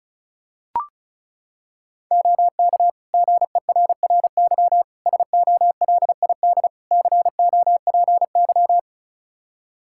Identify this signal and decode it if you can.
Morse code, heard as a keyed tone, about 34 wpm, 700 Hz. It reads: OK GERRY SOLID COPY
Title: Practice QSO - Exchange 3 - More Info - 35wpm